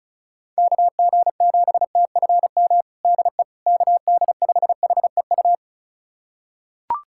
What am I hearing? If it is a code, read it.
Morse code, heard as a keyed tone, about 36 wpm, 700 Hz. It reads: KG7TFM DE KD5HEU
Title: USA Callsign DE Callsign - 6 Characters - 35wpm